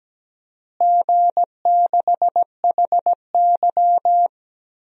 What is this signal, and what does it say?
Morse code, heard as a keyed tone, about 17 wpm, 700 Hz. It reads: G6HY
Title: DXCC-100 Call Signs - 4 Characters - Repeated 2x - 17wpm